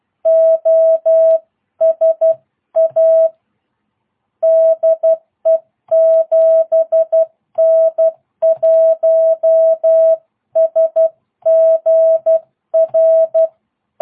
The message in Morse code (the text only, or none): OSA DE 7N1SGR